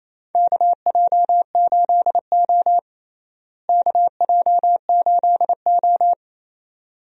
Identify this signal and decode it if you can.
Morse code, heard as a keyed tone, about 28 wpm, 700 Hz. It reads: KJ8O KJ8O
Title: USA Call Signs - 4 Characters - Repeated 2x - 28wpm